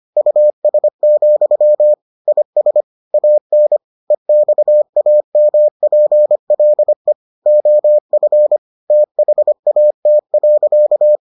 Morse code, heard as a keyed tone, about 25 wpm, 600 Hz. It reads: US, IS AN EXAMPLE OF THAT.